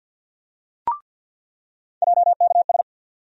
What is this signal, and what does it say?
Morse code, heard as a keyed tone, about 49 wpm, 700 Hz. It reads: JKS
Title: USA Call Sign Postfixes - 3 Letters 50wpm